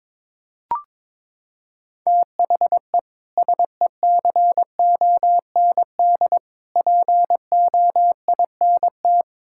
Morse code, heard as a keyed tone, about 22 wpm, 700 Hz. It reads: THE SECOND POINT
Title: Sentences from Top 1000 Words - Review Part 5 - 22wpm